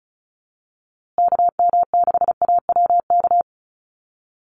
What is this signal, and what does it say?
Morse code, heard as a keyed tone, about 34 wpm, 700 Hz. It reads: KM6AWK
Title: USA Call Signs Repeated 2x 35wpm